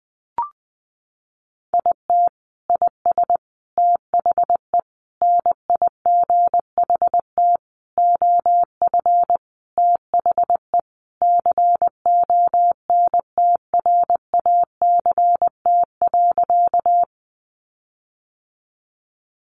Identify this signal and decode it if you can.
Morse code, heard as a keyed tone, about 20 wpm, 700 Hz. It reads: IT IS THE NIGHT OF THE CONTRACT.
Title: Sentences from Top 1400 Words 20wpm